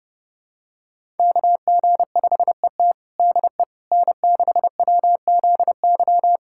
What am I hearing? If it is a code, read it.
Morse code, heard as a keyed tone, about 30 wpm, 700 Hz. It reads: KG5ET DE N6WZY
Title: USA Callsign DE Callsign - 5 Characters - 30wpm